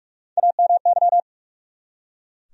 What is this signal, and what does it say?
Morse code, heard as a keyed tone, about 42 wpm, 700 Hz. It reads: AMY
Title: USA Common Names - Max 3 Letters - 45wpm